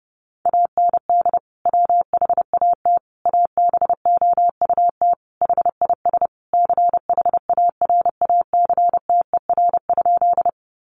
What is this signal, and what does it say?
Morse code, heard as a keyed tone, about 30 wpm, 700 Hz. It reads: AND WHAT ABOUT HIS CHARACTER?
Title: Sentences from Top 1200 Words 30wpm